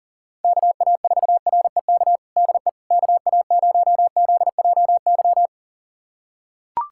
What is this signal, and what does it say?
Morse code, heard as a keyed tone, about 40 wpm, 700 Hz. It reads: KA4REK DE KA0ZJY
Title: USA Callsign DE Callsign 40wpm